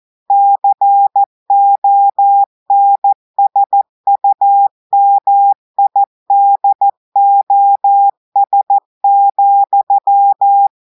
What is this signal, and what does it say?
Morse code, heard as a keyed tone, about 14 wpm, 800 Hz. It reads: CONSUMIDOS,
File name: audios / portugais / portugais texte en clair 2 vitesse 14 mots minutes